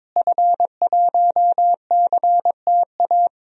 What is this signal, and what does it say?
Morse code, heard as a keyed tone, about 22 wpm, 700 Hz. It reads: F1CTA
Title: DXCC-100 Call Signs - 5 Characters - Repeated 2x - 22wpm